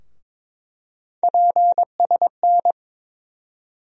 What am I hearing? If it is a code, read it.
Morse code, heard as a keyed tone, about 22 wpm, 700 Hz. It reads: PSN
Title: USA Call Sign Postfixes - 3 Letters 22wpm